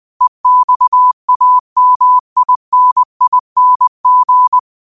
Morse code, heard as a keyed tone, about 20 wpm, 1000 Hz. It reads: EXAMINING
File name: qotdfiles / 20wpm